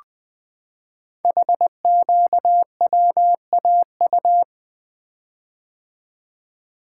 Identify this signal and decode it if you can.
Morse code, heard as a keyed tone, about 20 wpm, 700 Hz. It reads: HQWAU
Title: Code Groups - 5 Letters - 20wpm